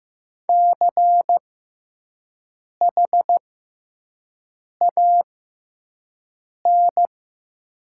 Morse code, heard as a keyed tone, about 15 wpm, spaced as for 5 wpm, 700 Hz. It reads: CHAN
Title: 6x Character Spacing - Top 200 Words - 15wpm